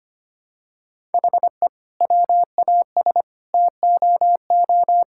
Morse code, heard as a keyed tone, about 25 wpm, 700 Hz. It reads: HE WAS TOO
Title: Sentences from Top 1000 Words - Review Part 6 - 25wpm